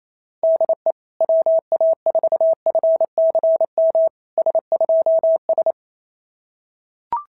Morse code, heard as a keyed tone, about 25 wpm, 650 Hz. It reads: DE WA4FCM S2S
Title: SOTA QSO Practice 28wpm